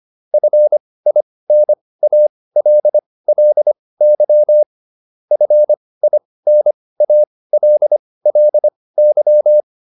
Morse code, heard as a keyed tone, about 25 wpm, spaced as for 18 wpm, 600 Hz. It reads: FINALLY FINALLY